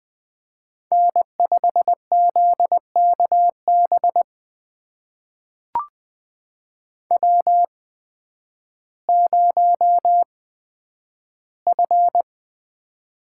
Morse code, heard as a keyed tone, about 20 wpm, spaced as for 11 wpm, 700 Hz. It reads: N5ZKB W0F
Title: USA Call Signs - 5 Characters - 8x Character Spacing - 20wpm